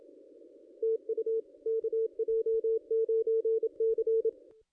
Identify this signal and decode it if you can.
Morse code, heard as a keyed tone, about 28 wpm, 440 Hz. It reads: TU KJ9C
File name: contests / ss-cw-k5tr-op-n5rz -1500